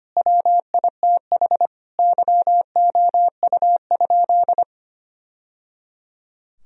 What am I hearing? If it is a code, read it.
Morse code, heard as a keyed tone, about 25 wpm, 700 Hz. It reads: WITH YOU?
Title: Sentences from Top 1000 Words - Review Part 1 - 25wpm